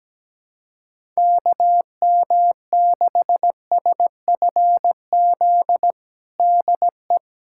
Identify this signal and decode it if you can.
Morse code, heard as a keyed tone, about 17 wpm, 700 Hz. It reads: KM6SFZ DE
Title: USA Callsign DE Callsign 17wpm